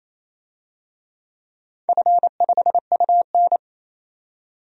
Morse code, heard as a keyed tone, about 28 wpm, 700 Hz. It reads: F5UN
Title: DXCC-100 Call Signs - 4 Characters - 28wpm